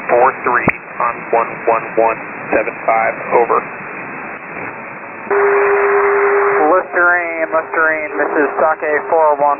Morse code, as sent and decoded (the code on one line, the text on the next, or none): none